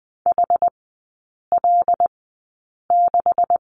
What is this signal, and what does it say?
Morse code, heard as a keyed tone, about 20 wpm, spaced as for 8 wpm, 700 Hz. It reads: HL6
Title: 138 Lesson 23e - Letter f - Three Characters 20wpm